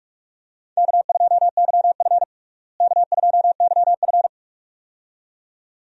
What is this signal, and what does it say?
Morse code, heard as a keyed tone, about 46 wpm, 700 Hz. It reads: K2YF K2YF
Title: USA Call Signs - 4 Characters - Repeated 2x - 45wpm